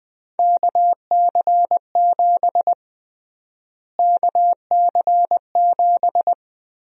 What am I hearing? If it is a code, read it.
Morse code, heard as a keyed tone, about 20 wpm, 700 Hz. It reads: KC7 KC7
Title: USA Call Sign Prefixes - Repeated 2x - 20wpm